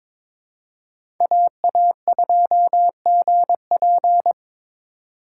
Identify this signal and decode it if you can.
Morse code, heard as a keyed tone, about 22 wpm, 700 Hz. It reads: AA2GP
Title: USA Call Signs - 5 Characters - 2x Character Spacing - 22wpm